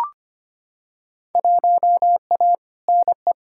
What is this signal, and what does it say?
Morse code, heard as a keyed tone, about 25 wpm, 700 Hz. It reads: 1A NE
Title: Field Day Exchanges Encore - 25wpm